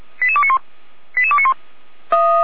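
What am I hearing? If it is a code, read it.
Five-tone selective call (ZVEI), heard as quick runs of five tones, tones of about 70 ms. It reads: 88291 88291
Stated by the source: ZVEI five-tone selective call received by SDR.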